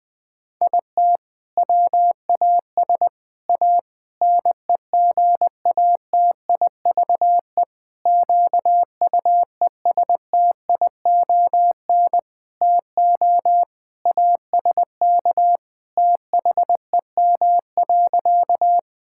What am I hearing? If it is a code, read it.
Morse code, heard as a keyed tone, about 21 wpm, 700 Hz. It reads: IT WAS A NEGATIVE QUESTION TO ASK THEM.
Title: Sentences from Top 1000 Words - Review Part 6 - 20wpm